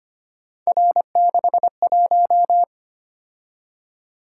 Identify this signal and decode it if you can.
Morse code, heard as a keyed tone, about 25 wpm, 700 Hz. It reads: R61
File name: Three Letter-Number Combinations - 25wpm